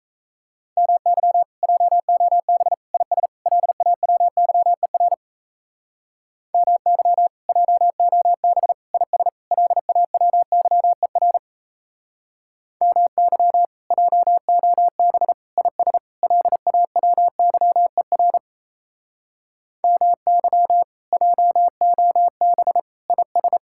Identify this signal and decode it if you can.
Morse code, heard as a keyed tone, about 36 wpm, 700 Hz. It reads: MY JOB IS LAWYER MY JOB IS LAWYER MY JOB IS LAWYER MY JOB IS